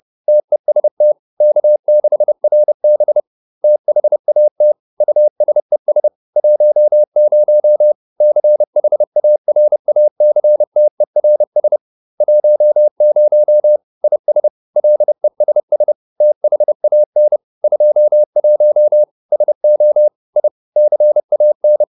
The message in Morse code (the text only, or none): TEST K6RB THAT USES 10 CHARACTERS 10 IS LESS THAN 21 SO I CAN